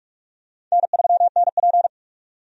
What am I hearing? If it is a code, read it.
Morse code, heard as a keyed tone, about 42 wpm, 700 Hz. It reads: N3NP
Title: USA Call Signs - 4 Characters - 45wpm